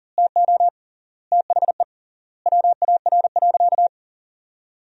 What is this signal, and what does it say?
Morse code, heard as a keyed tone, about 37 wpm, 700 Hz. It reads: TO THE WAR.
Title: 3x Word-Spacing - Sentences from Top 400 Words - 40wpm